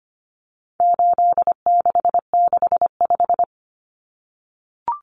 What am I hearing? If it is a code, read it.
Morse code, heard as a keyed tone, about 25 wpm, 700 Hz. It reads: 8665
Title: Numbers - 4-Digit 25wpm